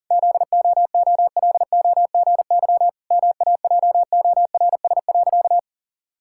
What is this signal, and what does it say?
Morse code, heard as a keyed tone, about 40 wpm, 700 Hz. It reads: ZOOLOGY MAJORS.
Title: Pangram Challenge v2 - 40wpm